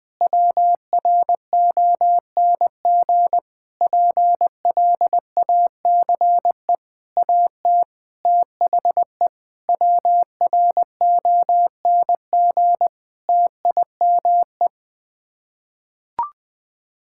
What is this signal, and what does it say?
Morse code, heard as a keyed tone, about 20 wpm, 700 Hz. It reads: WRONG PLACE AT THE WRONG TIME
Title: Sets of 8 Words - 20wpm